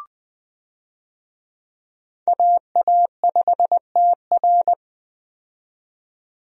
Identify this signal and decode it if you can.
Morse code, heard as a keyed tone, about 21 wpm, 700 Hz. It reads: AA5TR
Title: USA Call Signs - 5 Characters - 20wpm